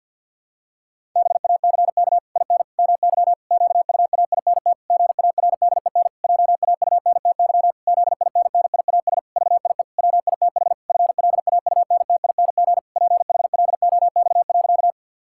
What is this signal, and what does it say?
Morse code, heard as a keyed tone, about 50 wpm, 700 Hz. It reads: BACK IN MY QUAINT GARDEN JAUNTY ZINNIAS VIE WITH FLAUNTING PHLOX.
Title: Pangram Challenge v2 - 50wpm